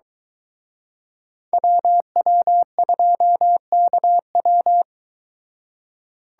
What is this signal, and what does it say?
Morse code, heard as a keyed tone, about 22 wpm, 700 Hz. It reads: WW2KW